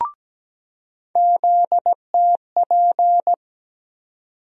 Morse code, heard as a keyed tone, about 17 wpm, 700 Hz. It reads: ZTP
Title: USA Call Sign Postfixes - 3 Letters 17wpm